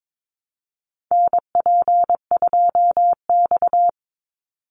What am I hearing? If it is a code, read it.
Morse code, heard as a keyed tone, about 22 wpm, 700 Hz. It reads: NP2X
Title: Call Signs Repeated 2x - 22wpm